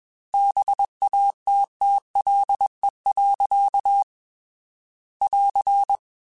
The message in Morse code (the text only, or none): BATTLE. <AR>